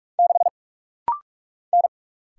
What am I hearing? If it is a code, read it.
Morse code, heard as a keyed tone, about 44 wpm, 700 Hz. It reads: 6 N
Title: Single Letter-Number - Rapid-Fire 45wpm